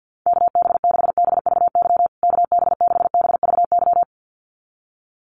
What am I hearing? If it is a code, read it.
Morse code, heard as a keyed tone, about 50 wpm, 700 Hz. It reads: KB6BVY KB6BVY
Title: USA Call Signs Repeated 2x 50wpm